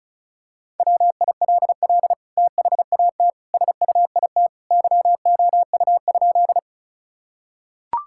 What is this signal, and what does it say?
Morse code, heard as a keyed tone, about 36 wpm, 700 Hz. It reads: WILL THAT SUIT YOU?
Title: Sentences from Top 1600 Words 35wpm